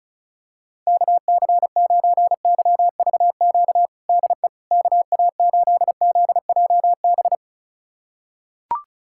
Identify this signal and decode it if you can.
Morse code, heard as a keyed tone, about 34 wpm, 700 Hz. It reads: KC9YVQ DE KA8ZJB
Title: USA Callsign DE Callsign - 6 Characters - 35wpm